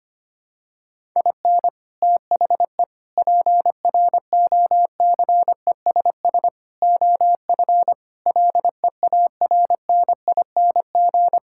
Morse code, heard as a keyed tone, about 25 wpm, 700 Hz. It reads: IN THE PROCESS OF LEARNING
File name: practice / 25wpm / Sentences with 4-Word Phrases - 25wpm